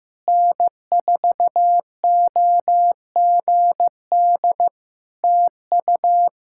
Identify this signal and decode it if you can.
Morse code, heard as a keyed tone, about 15 wpm, 700 Hz. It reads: N4OGD TU
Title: Practice QSO - Exchange 4 - Ending - 15wpm